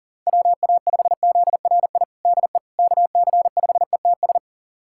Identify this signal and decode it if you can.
Morse code, heard as a keyed tone, about 40 wpm, 700 Hz. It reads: WA5ZRI DE KC5ETS
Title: USA Callsign DE Callsign 40wpm